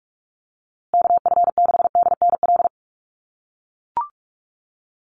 Morse code, heard as a keyed tone, about 44 wpm, 690 Hz. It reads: KF6DNL
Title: USA Call Signs Repeated 3x 45wpm